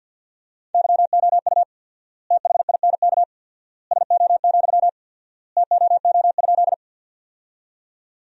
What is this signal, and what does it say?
Morse code, heard as a keyed tone, about 50 wpm, 700 Hz. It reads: YOU THINK SO, TOO?
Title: 4x Word-Spacing - Sentences from Top 200 Words - 50wpm